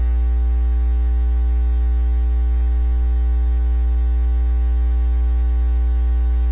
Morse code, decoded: none

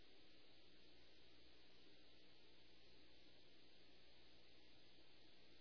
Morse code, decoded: none

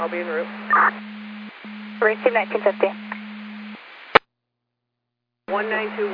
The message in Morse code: none

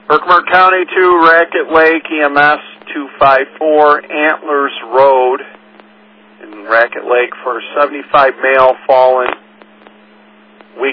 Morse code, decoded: none

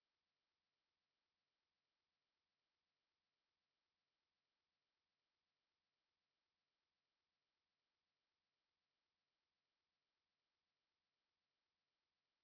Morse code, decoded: none

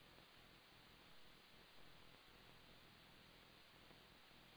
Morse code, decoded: none

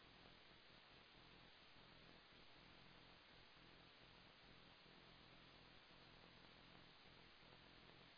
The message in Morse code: none

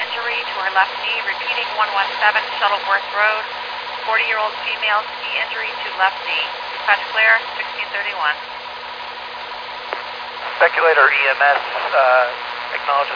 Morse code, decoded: none

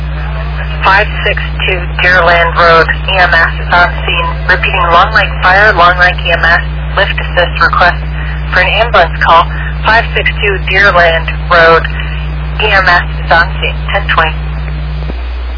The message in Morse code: none

